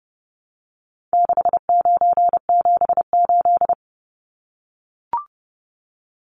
Morse code, heard as a keyed tone, about 30 wpm, 700 Hz. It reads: 6978
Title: Numbers - 4-Digit 30wpm